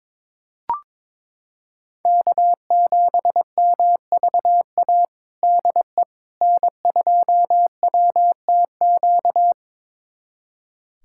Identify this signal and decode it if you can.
Morse code, heard as a keyed tone, about 21 wpm, 700 Hz. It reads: K7MVA DE N2WTQ
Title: USA Callsign DE Callsign - 5 Characters - 22wpm